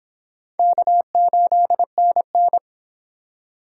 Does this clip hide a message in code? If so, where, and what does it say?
Yes: Morse code from 0.6 to 2.6 s, heard as a keyed tone, about 25 wpm, 700 Hz. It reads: K8NN